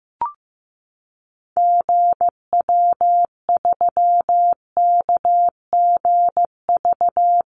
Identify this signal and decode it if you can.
Morse code, heard as a keyed tone, about 15 wpm, 700 Hz. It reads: GW3KGV
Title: Call Signs Repeated 3x - 15wpm